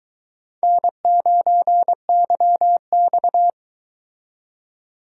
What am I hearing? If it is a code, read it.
Morse code, heard as a keyed tone, about 22 wpm, 700 Hz. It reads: N9YX